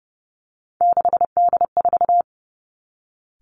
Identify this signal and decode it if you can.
Morse code, heard as a keyed tone, about 30 wpm, 700 Hz. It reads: 6D4
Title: Three Challenging Letter-Number Combinations 30wpm